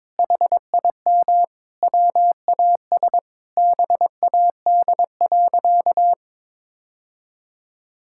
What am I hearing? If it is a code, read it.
Morse code, heard as a keyed tone, about 22 wpm, 700 Hz. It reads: HIM WAS BAD.
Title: Sentences from Top 1600 Words 22wpm